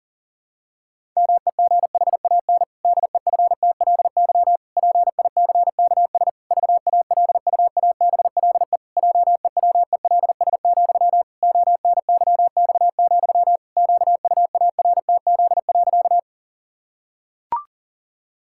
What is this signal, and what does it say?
Morse code, heard as a keyed tone, about 38 wpm, 700 Hz. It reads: MEGHAN DEFTLY PICKS VALUABLE JEWELS, ONYX, QUARTZ.
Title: Pangram Challenge v2 - 40wpm